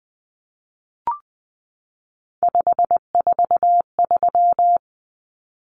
Morse code, heard as a keyed tone, about 20 wpm, 700 Hz. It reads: 543
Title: Numbers - 3-Digit 20wpm